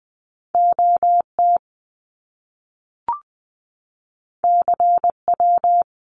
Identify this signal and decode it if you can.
Morse code, heard as a keyed tone, about 20 wpm, 700 Hz. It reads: OT CW
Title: Abbreviations Repeated 3x - 20wpm